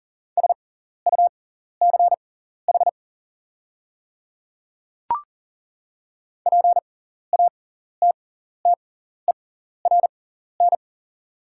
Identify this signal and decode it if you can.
Morse code, heard as a keyed tone, about 40 wpm, spaced as for 14 wpm, 700 Hz. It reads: SUCH PATTERN
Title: 6x Character Spacing - Top 500 Words - 40wpm